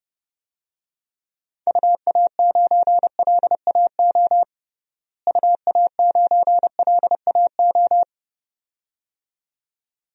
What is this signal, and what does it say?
Morse code, heard as a keyed tone, about 30 wpm, 700 Hz. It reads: UA9LAO UA9LAO
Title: DXCC-100 Call Signs - 6 Characters - Repeated 2x - 30wpm